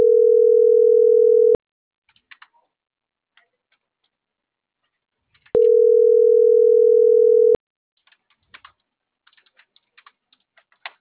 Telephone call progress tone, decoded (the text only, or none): ringback tone